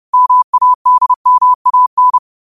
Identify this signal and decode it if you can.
Morse code, heard as a keyed tone, about 30 wpm, 1000 Hz. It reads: MADMAN
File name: 30wpm